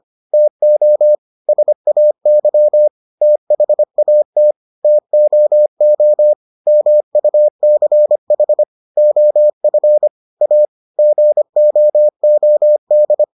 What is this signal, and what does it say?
Morse code, heard as a keyed tone, about 25 wpm, 600 Hz. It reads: TO SAY THAT TOO MUCH OF A GOOD